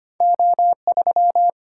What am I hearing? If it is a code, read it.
Morse code, heard as a keyed tone, about 25 wpm, 700 Hz. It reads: O3
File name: Two Letter-Number Combinations - 25wpm